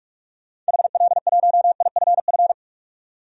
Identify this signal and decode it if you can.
Morse code, heard as a keyed tone, about 46 wpm, 700 Hz. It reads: HL1IFF
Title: DXCC-100 Call Signs - 6 Characters - 45wpm